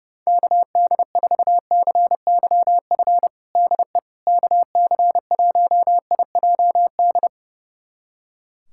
Morse code, heard as a keyed tone, about 29 wpm, 700 Hz. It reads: KD4CYF DE KC1IJD
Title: USA Callsign DE Callsign - 6 Characters - 30wpm